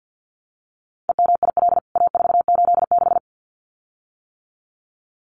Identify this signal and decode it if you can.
Morse code, heard as a keyed tone, about 50 wpm, 700 Hz. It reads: EMIL A486
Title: CWT Exchange 50wpm